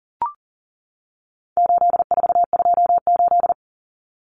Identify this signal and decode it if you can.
Morse code, heard as a keyed tone, about 40 wpm, 700 Hz. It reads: 8428
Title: Numbers - 4-Digit 40wpm